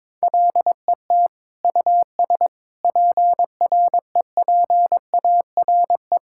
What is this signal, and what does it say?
Morse code, heard as a keyed tone, about 22 wpm, 700 Hz. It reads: LET US PREPARE
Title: Sentences from Top 1000 Words - Review Part 6 - 22wpm